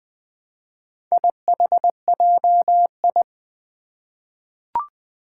Morse code, heard as a keyed tone, about 20 wpm, 700 Hz. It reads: IHJI
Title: Four Challenging Letter-Number Combinations 20wpm